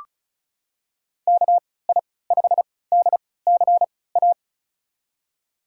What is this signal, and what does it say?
Morse code, heard as a keyed tone, about 34 wpm, spaced as for 20 wpm, 700 Hz. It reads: KI5DCA
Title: USA Call Signs - 6 Characters - 3x Character Spacing - 35wpm